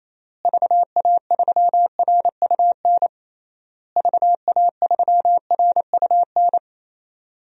Morse code, heard as a keyed tone, about 28 wpm, 700 Hz. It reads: VA3RUN VA3RUN
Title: DXCC-100 Call Signs - 6 Characters - Repeated 3x - 28wpm